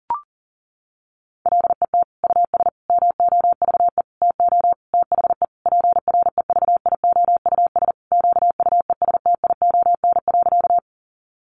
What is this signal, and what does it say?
Morse code, heard as a keyed tone, about 40 wpm, 700 Hz. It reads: LET US MOVE TO THE PREVIOUS QUESTION.
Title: Sentences from Top 1500 Words 40wpm